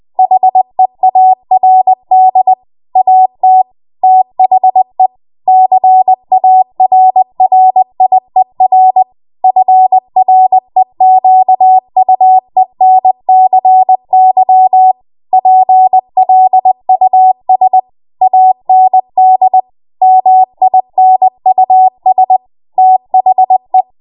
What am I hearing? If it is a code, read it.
Morse code, heard as a keyed tone, about 20 wpm, 750 Hz. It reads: HEARD AT THE CARRIER FREQUENCY PLUS AND MINUS THE
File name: Morse / 20WPM